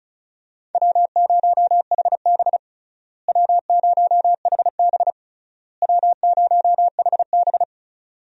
Morse code, heard as a keyed tone, about 34 wpm, 700 Hz. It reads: W0HB W0HB W0HB
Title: USA Call Signs - 4 Characters - Repeated 3x - 35wpm